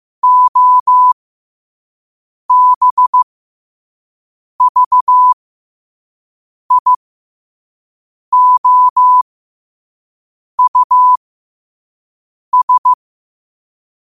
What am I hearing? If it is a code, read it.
Morse code, heard as a keyed tone, about 15 wpm, spaced as for 5 wpm, 1000 Hz. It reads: OBVIOUS
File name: qotdfiles / 5wpm